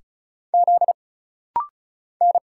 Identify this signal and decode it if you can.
Morse code, heard as a keyed tone, about 36 wpm, 700 Hz. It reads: Z N
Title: Single Letter-Number Alt - Rapid-Fire 35wpm